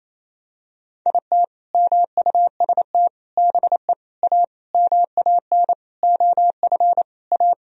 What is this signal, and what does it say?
Morse code, heard as a keyed tone, about 28 wpm, 700 Hz. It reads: IT MUST BE A MAN OF A
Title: Sentences from Top 1000 Words - Review Part 3 - 28wpm